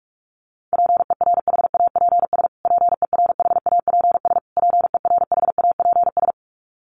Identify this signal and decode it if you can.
Morse code, heard as a keyed tone, about 44 wpm, 700 Hz. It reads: PERHAPS PERHAPS PERHAPS
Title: Top 500 Words Repeated 45wpm